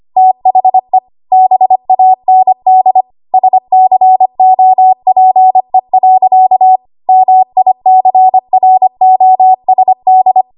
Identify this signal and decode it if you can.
Morse code, heard as a keyed tone, about 25 wpm, 750 Hz. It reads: THE BAND SCOPE. MICROSD
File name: Archive / 25 WPM / 25WPM